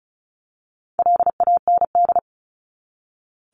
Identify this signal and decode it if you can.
Morse code, heard as a keyed tone, about 34 wpm, 700 Hz. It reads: LAND
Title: Top 300 Words 35wpm